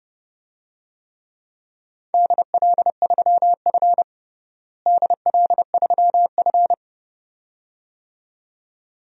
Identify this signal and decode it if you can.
Morse code, heard as a keyed tone, about 30 wpm, 700 Hz. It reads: DL3F DL3F
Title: DXCC-100 Call Signs - 4 Characters - Repeated 2x - 30wpm